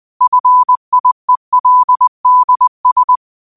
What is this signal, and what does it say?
Morse code, heard as a keyed tone, about 20 wpm, 1000 Hz. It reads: FIELDS